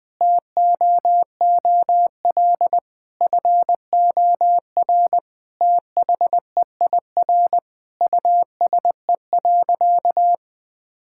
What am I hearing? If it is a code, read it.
Morse code, heard as a keyed tone, about 21 wpm, 700 Hz. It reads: TOOL FOR THEIR USE.
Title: Sentences from Top 1000 Words - Review Part 6 - 20wpm